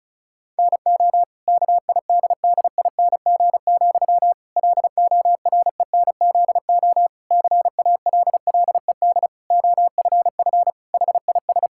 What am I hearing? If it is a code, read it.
Morse code, heard as a keyed tone, about 34 wpm, 700 Hz. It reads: NO KIDDING, LORENZO CALLED OFF HIS
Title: Pangram Challenge v2 - 35wpm